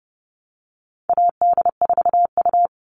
Morse code, heard as a keyed tone, about 30 wpm, 700 Hz. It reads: AD4U
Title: All Cut Numbers - 4-Digit 30wpm